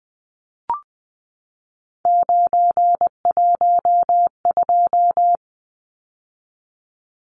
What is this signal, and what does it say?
Morse code, heard as a keyed tone, about 20 wpm, 700 Hz. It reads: 912
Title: Numbers - 3-Digit 20wpm